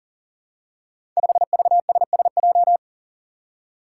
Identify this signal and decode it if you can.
Morse code, heard as a keyed tone, about 42 wpm, 700 Hz. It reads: 5VSSJ
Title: Five Challenging Letter-Number Combinations 40wpm